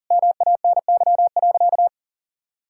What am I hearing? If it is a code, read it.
Morse code, heard as a keyed tone, about 40 wpm, 700 Hz. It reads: MANY.
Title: Sentences from Top 1000 Words - Review Part 6 - 40wpm